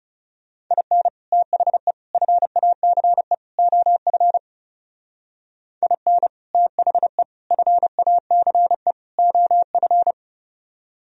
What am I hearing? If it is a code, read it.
Morse code, heard as a keyed tone, about 38 wpm, 700 Hz. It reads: IN THE FACE OF IN THE FACE OF